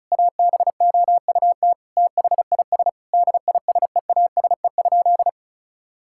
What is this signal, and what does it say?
Morse code, heard as a keyed tone, about 36 wpm, 700 Hz. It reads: ABOUT THIS DISEASE?
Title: Sentences from Top 1000 Words - Review Part 6 - 35wpm